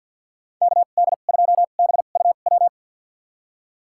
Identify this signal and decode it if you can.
Morse code, heard as a keyed tone, about 47 wpm, spaced as for 35 wpm, 700 Hz. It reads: KD2BUW
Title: USA Call Signs - 2x Character Spacing - 50wpm